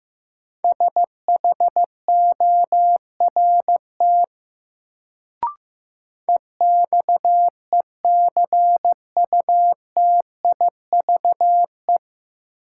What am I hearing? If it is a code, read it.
Morse code, heard as a keyed tone, about 15 wpm, 700 Hz. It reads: SHORT EXECUTIVE
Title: Sets of 5 Words Encore - New Word Drill - 15wpm